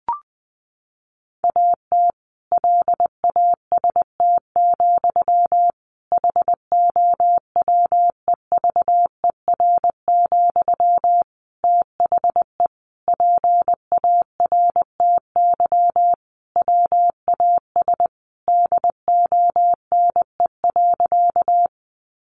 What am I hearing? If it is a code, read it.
Morse code, heard as a keyed tone, about 20 wpm, 700 Hz. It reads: AT LAST, HOWEVER, THE PARTY WAS DONE.
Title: Sentences from Top 1400 Words 20wpm